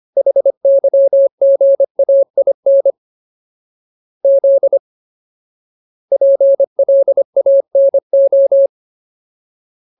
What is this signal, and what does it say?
Morse code, heard as a keyed tone, about 25 wpm, 550 Hz. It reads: HYGAIN Z PLANO
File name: Ultimate Word List / 25WPM Ultimate